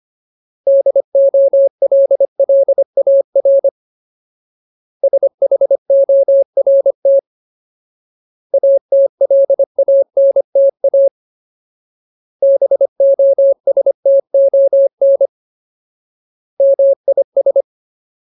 Morse code, heard as a keyed tone, about 25 wpm, 550 Hz. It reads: DOLLAR SHORT ATLANTA BOSTON MIS
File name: Ultimate Word List / 25WPM Ultimate